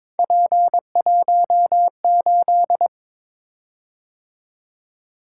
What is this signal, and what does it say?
Morse code, heard as a keyed tone, about 23 wpm, 700 Hz. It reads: P18
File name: practice / 22wpm / Three Letter-Number Combinations - 22wpm